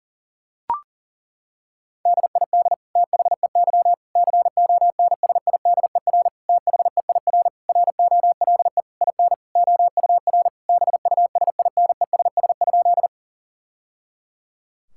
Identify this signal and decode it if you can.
Morse code, heard as a keyed tone, about 40 wpm, 700 Hz. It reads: DID THEY CONSIDER THEIR ROLE IN OUR BUSINESS?
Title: Sentences from Top 1000 Words - Review Part 6 - 40wpm